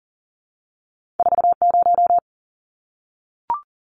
Morse code, heard as a keyed tone, about 38 wpm, 700 Hz. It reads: 40
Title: Numbers - 2-Digit 40wpm